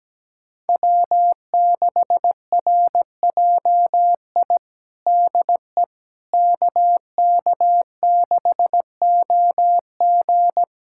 Morse code, heard as a keyed tone, about 17 wpm, 700 Hz. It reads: W6RJI DE KK6OG
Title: USA Callsign DE Callsign - 5 Characters - 17wpm